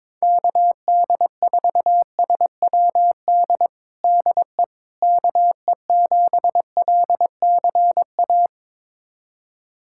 Morse code, heard as a keyed tone, about 22 wpm, 700 Hz. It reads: KD4SWD DE KE7LCA
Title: USA Callsign DE Callsign - 6 Characters - 22wpm